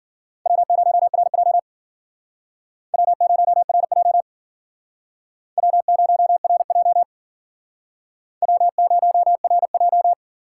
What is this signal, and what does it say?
Morse code, heard as a keyed tone, about 51 wpm, 700 Hz. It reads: W0RJ W0RJ W0RJ W0RJ